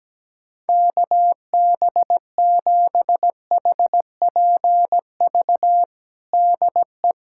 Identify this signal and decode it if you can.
Morse code, heard as a keyed tone, about 17 wpm, 700 Hz. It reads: KB7HPV DE
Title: USA Callsign DE Callsign 17wpm